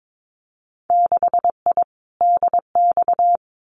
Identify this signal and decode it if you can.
Morse code, heard as a keyed tone, about 22 wpm, 700 Hz. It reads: 6I DX
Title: Winter Field Day Exchanges Encore 22wpm